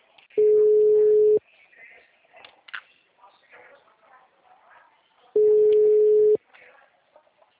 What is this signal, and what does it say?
Ringback tone (Europe, 425 Hz).
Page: Cordeiropolis